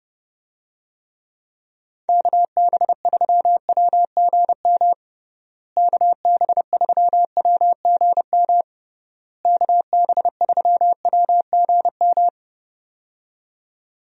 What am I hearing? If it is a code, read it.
Morse code, heard as a keyed tone, about 32 wpm, 700 Hz. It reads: KB3WGM KB3WGM KB3WGM
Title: USA Call Signs - 6 Characters - Repeated 3x - 30wpm